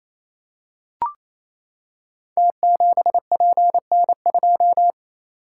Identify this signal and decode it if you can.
Morse code, heard as a keyed tone, about 27 wpm, 700 Hz. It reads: T7PN2
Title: Code Groups - 5 Letter:Numbers - 28wpm